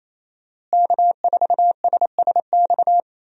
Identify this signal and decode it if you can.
Morse code, heard as a keyed tone, about 27 wpm, 700 Hz. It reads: K4SSX
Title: USA Call Signs - 5 Characters - Repeated 3x - 28wpm